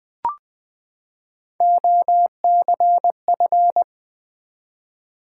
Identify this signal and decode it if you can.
Morse code, heard as a keyed tone, about 20 wpm, 700 Hz. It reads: OCF
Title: Top 180 QSO Elements 20wpm